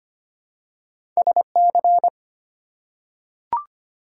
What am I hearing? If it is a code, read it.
Morse code, heard as a keyed tone, about 25 wpm, 700 Hz. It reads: SC
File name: practice / 25wpm / ARRL Sections v2 - 25wpm